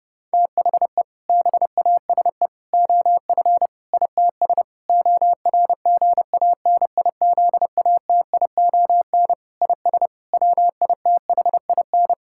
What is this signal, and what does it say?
Morse code, heard as a keyed tone, about 30 wpm, 700 Hz. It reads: THE BASE OF ITS ORGANIZATION IS WITHIN
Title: Sentences from Top 1000 Words - Review Part 5 - 30wpm